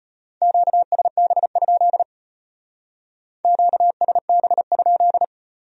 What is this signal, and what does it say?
Morse code, heard as a keyed tone, about 33 wpm, 700 Hz. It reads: QSB? QSB?